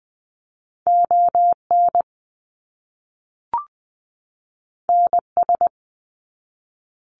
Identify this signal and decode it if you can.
Morse code, heard as a keyed tone, about 20 wpm, 700 Hz. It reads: ON NS
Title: Canadian Provinces and Territories 20wpm